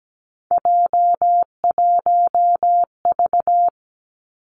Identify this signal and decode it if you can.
Morse code, heard as a keyed tone, about 17 wpm, 700 Hz. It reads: J1V
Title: Three Challenging Letter-Number Combinations 17wpm